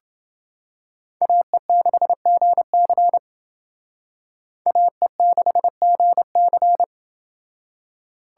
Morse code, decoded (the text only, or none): AE6GC AE6GC